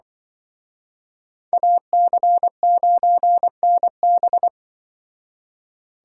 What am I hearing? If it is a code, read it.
Morse code, heard as a keyed tone, about 25 wpm, 700 Hz. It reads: AC9NB
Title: Speed Racing - USA 2x2 Call Signs 20wpm